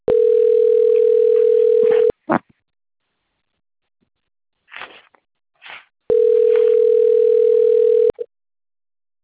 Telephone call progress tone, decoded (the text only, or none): ringback tone